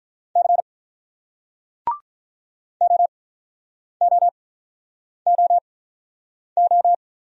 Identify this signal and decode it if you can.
Morse code, heard as a keyed tone, about 47 wpm, spaced as for 9 wpm, 700 Hz. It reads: C OOOO